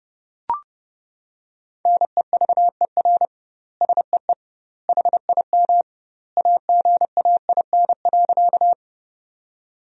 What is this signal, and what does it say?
Morse code, heard as a keyed tone, about 30 wpm, 700 Hz. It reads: NEVER SEE HIM AGAIN.
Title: 2x Word-Spacing - Sentences from Top 300 Words - 30wpm